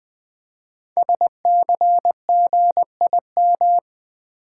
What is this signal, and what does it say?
Morse code, heard as a keyed tone, about 20 wpm, 700 Hz. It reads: SCGIM
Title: Code Groups - 5 Letter:Numbers - 20wpm